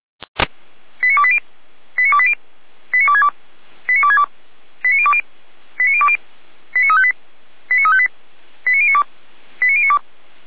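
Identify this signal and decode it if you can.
Five-tone selective call (ZVEI), heard as quick runs of five tones, tones of about 70 ms. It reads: 89280 89280 89262 89262 89020 89020 89358 89358 89092 89092
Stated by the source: ZVEI five-tone call 89320 received via SDR.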